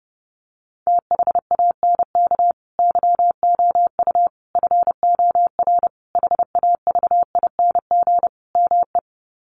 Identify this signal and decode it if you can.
Morse code, heard as a keyed tone, about 30 wpm, 700 Hz. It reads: THANK YOU FOR HAVING ME
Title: Sets of 5 Words Encore 30wpm